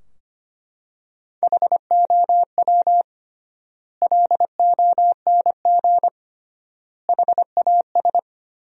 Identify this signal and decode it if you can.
Morse code, heard as a keyed tone, about 25 wpm, 700 Hz. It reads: HOW LONG HAS
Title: 3x Word-Spacing - Sentences from Top 100 Words - 25wpm